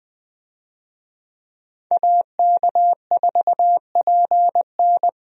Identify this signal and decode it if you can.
Morse code, heard as a keyed tone, about 20 wpm, 700 Hz. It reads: AK4PN
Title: USA Call Signs - 5 Characters - Repeated 2x - 20wpm